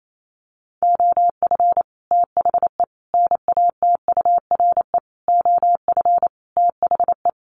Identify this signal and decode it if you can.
Morse code, heard as a keyed tone, about 28 wpm, 700 Hz. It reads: OF THE NATURE OF THE
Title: Sets of 5 Words Encore2 28wpm